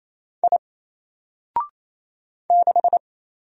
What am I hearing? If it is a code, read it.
Morse code, heard as a keyed tone, about 28 wpm, 700 Hz. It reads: I 6
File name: Single-Letter-Number-28wpm